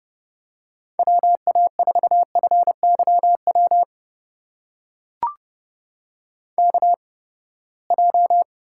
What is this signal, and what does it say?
Morse code, heard as a keyed tone, about 32 wpm, spaced as for 21 wpm, 700 Hz. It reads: WA4FYW KJ
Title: USA Call Signs - 6 Characters - 8x Character Spacing - 30wpm